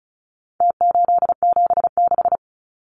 Morse code, heard as a keyed tone, about 34 wpm, 700 Hz. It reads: T876
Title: Numbers - Contest Style - 4-Digit 35wpm